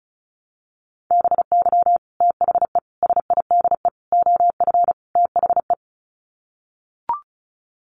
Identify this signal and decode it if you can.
Morse code, heard as a keyed tone, about 34 wpm, 700 Hz. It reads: BY THE SIDE OF THE
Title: Sets of 5 Words Encore2 35wpm